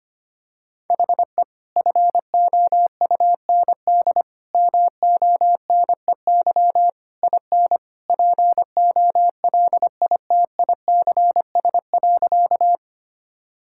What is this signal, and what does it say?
Morse code, heard as a keyed tone, about 25 wpm, 700 Hz. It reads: HE FOUND MONEY IN POLITICS.
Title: Sentences from Top 1000 Words - Review Part 6 - 25wpm